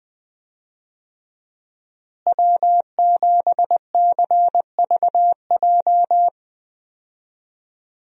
Morse code, heard as a keyed tone, about 20 wpm, 700 Hz. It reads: W7CVJ
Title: USA Call Signs - 5 Characters - Repeated 2x - 20wpm